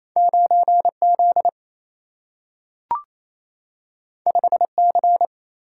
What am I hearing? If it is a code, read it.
Morse code, heard as a keyed tone, about 28 wpm, 700 Hz. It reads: 9Z 5C
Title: Single-Letter Single-Number 28wpm